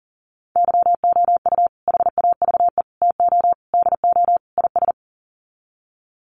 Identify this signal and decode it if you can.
Morse code, heard as a keyed tone, about 40 wpm, 700 Hz. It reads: YOU HAVE TO DO IS
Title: Sets of 5 Words 40wpm